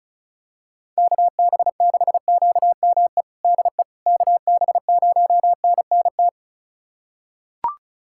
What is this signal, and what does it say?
Morse code, heard as a keyed tone, about 34 wpm, 700 Hz. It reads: KB6QME DE KB0NNT
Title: USA Callsign DE Callsign - 6 Characters - 35wpm